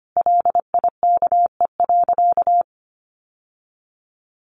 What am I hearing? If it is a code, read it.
Morse code, heard as a keyed tone, about 25 wpm, 700 Hz. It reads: LIKE.
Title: Sentences from Top 1400 Words 25wpm